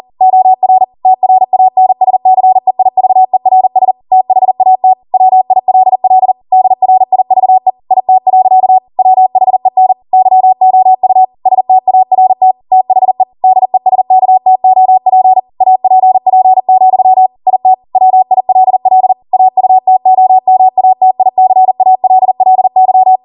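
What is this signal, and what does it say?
Morse code, heard as a keyed tone, about 40 wpm, 750 Hz. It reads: OR TRANSCEIVERS THAT WILL DRIVE IT. WHEN YOU START THE DESKTOP APP, IT WILL AUTOMATICALLY
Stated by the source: Code practice at 40 wpm.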